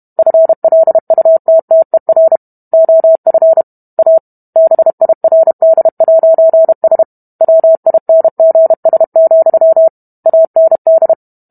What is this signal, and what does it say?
Morse code, heard as a keyed tone, about 32 wpm, 650 Hz. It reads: FLUTTER OF A BIRD'S WINGS, AND